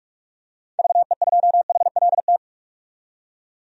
Morse code, heard as a keyed tone, about 43 wpm, 700 Hz. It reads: VE2HLT
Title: DXCC-100 Call Signs - 6 Characters - 45wpm